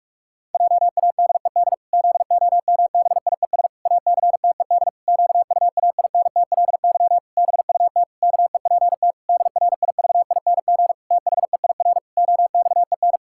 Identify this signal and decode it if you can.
Morse code, heard as a keyed tone, about 46 wpm, 700 Hz. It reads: JADED ZOMBIES ACTED QUAINTLY BUT KEPT DRIVING THEIR OXEN
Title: Pangram Challenge v2 - 45wpm